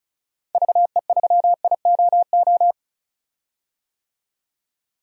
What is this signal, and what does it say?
Morse code, heard as a keyed tone, about 36 wpm, 700 Hz. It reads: VE3IOO
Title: DXCC-100 Call Signs - 6 Characters - 35wpm